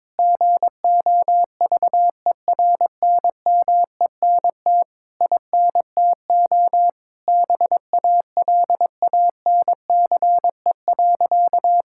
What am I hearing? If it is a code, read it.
Morse code, heard as a keyed tone, about 22 wpm, 700 Hz. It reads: GOVERNMENT INTO BALANCE.
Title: Sentences from Top 1000 Words - Review Part 6 - 22wpm